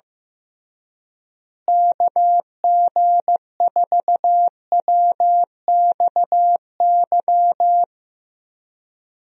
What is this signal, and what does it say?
Morse code, heard as a keyed tone, about 15 wpm, 700 Hz. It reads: KG4WXY
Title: USA Call Signs - 6 Characters - Repeated 3x - 15wpm